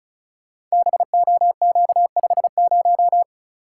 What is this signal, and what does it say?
Morse code, heard as a keyed tone, about 36 wpm, 690 Hz. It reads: BOQ50
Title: Code Groups - 5 Letter:Numbers - 35wpm